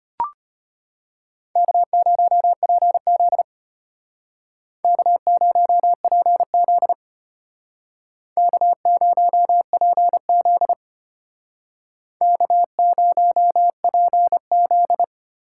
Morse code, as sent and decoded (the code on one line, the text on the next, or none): -.- ----- .--. --.. / -.- ----- .--. --.. / -.- ----- .--. --.. / -.- ----- .--. --..
K0PZ K0PZ K0PZ K0PZ